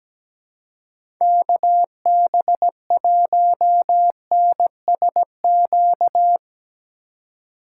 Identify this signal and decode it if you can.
Morse code, heard as a keyed tone, about 17 wpm, 700 Hz. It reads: KB1NSQ
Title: USA Call Signs - 2x Character Spacing - 17wpm